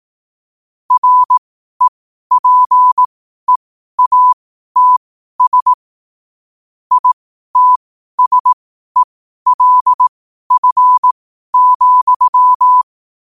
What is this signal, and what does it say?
Morse code, heard as a keyed tone, about 18 wpm, spaced as for 12 wpm, 1000 Hz. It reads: REPEATS ITSELF,
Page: New quotes every day in morse code at 12 Words per minute.